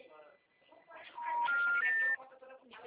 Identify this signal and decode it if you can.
Special information tone, with speech.